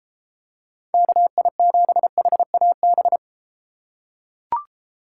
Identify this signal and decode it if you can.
Morse code, heard as a keyed tone, about 32 wpm, 700 Hz. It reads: KI7HAB